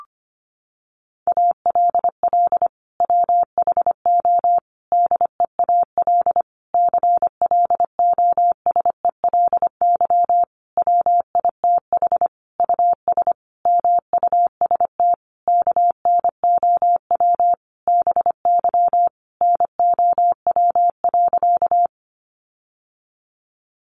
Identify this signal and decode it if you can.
Morse code, heard as a keyed tone, about 25 wpm, 700 Hz. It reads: ALL WHO DEAL CLOSELY WITH US MUST KNOW BY NOW.
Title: Sentences from Top 1400 Words 25wpm